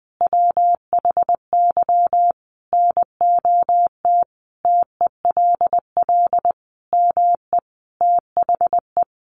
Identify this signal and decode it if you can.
Morse code, heard as a keyed tone, about 20 wpm, 700 Hz. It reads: WHY NOT TELL ME THE
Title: Sentences from Top 800 Words 20wpm